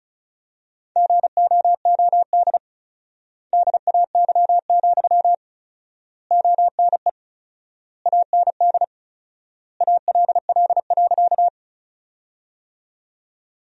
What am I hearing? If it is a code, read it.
Morse code, heard as a keyed tone, about 34 wpm, 700 Hz. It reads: GOOD DAY, ONE AND ALL.
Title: 4x Word-Spacing - Sentences from Top 200 Words - 35wpm